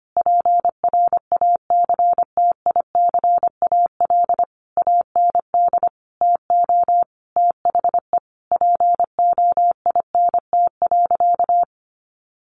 Morse code, heard as a keyed tone, about 25 wpm, 700 Hz. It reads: PRACTICAL AND TO THE POINT.
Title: Sentences from Top 1400 Words 25wpm